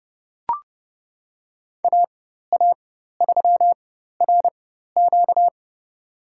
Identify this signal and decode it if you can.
Morse code, heard as a keyed tone, about 30 wpm, spaced as for 14 wpm, 700 Hz. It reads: AA3RQ
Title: USA Call Signs - 5 Characters - 4x Character Spacing - 30wpm